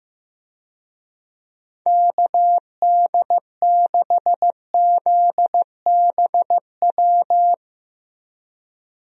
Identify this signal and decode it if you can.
Morse code, heard as a keyed tone, about 15 wpm, 700 Hz. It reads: KD6ZBW
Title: USA Call Signs - 6 Characters - 15wpm